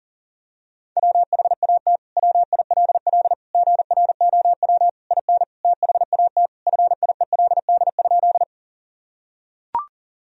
Morse code, heard as a keyed tone, about 37 wpm, 700 Hz. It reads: WHAT WILL GROW IN THAT FIELD?
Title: Sentences from Top 1000 Words - Review Part 2 - 40wpm